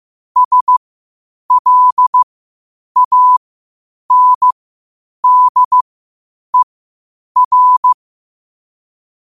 Morse code, heard as a keyed tone, about 15 wpm, spaced as for 8 wpm, 1000 Hz. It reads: SLANDER
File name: qotdfiles / 8wpm